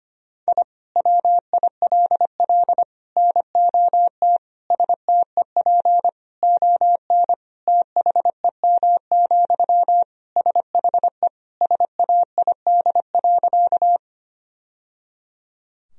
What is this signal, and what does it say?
Morse code, heard as a keyed tone, about 25 wpm, 700 Hz. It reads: I WILL NOT STEP ON THEM, SHE SAID.
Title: Sentences from Top 1000 Words - Review Part 2 - 25wpm